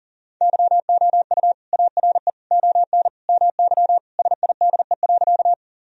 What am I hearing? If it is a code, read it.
Morse code, heard as a keyed tone, about 40 wpm, 700 Hz. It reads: YOU ARE ON MY SIDE.
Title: Sentences from Top 1000 Words - Review Part 1 - 40wpm